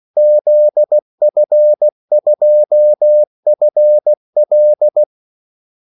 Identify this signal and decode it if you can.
Morse code, heard as a keyed tone, about 16 wpm, 600 Hz. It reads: ZF2FL
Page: Kaldesignaler 16 wpm